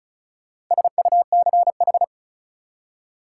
Morse code, heard as a keyed tone, about 34 wpm, 700 Hz. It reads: SUCH
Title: Four Letter Words - 35wpm